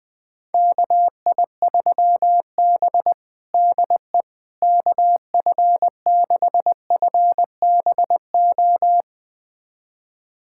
Morse code, heard as a keyed tone, about 21 wpm, 700 Hz. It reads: KI3B DE KF6FBO
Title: USA Callsign DE Callsign 20wpm